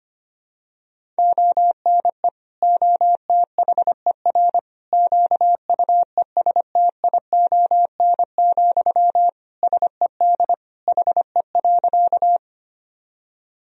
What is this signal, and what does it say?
Morse code, heard as a keyed tone, about 25 wpm, 700 Hz. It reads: ONE OTHER QUESTION, SED HE.
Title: Sentences from Top 400 Words and Abbreviations 25wpm